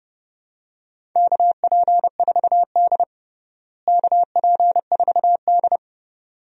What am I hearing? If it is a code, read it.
Morse code, heard as a keyed tone, about 30 wpm, 700 Hz. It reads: KP4D KP4D
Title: DXCC-100 Call Signs - 4 Characters - Repeated 2x - 30wpm